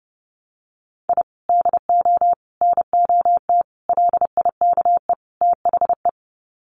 Morse code, heard as a keyed tone, about 30 wpm, 700 Hz. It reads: I DO NOT LIKE THE
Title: Sets of 5 Words Encore 30wpm